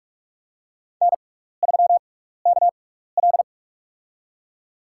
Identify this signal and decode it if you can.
Morse code, heard as a keyed tone, about 44 wpm, spaced as for 15 wpm, 700 Hz. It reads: N3KL
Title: USA Call Signs - 4 Characters - 6x Character Spacing - 45wpm